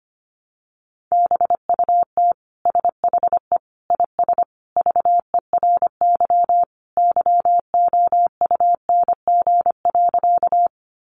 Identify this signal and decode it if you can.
Morse code, heard as a keyed tone, about 25 wpm, 700 Hz. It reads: BUT SHE IS VERY YOUNG.
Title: Sentences from Top 400 Words 25wpm